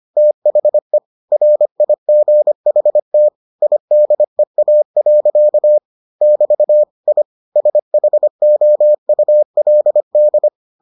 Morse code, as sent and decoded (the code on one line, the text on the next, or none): - .... / .-. .. --. .... - / .. -.. .- .-.-.- / -...- / .. / ... .... --- ..- .-.. -..
THE RIGHT IDEA. <BT> I SHOULD